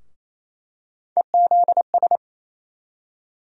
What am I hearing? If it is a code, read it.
Morse code, heard as a keyed tone, about 28 wpm, 700 Hz. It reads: EZS
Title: USA Call Sign Postfixes - 3 Letters 28wpm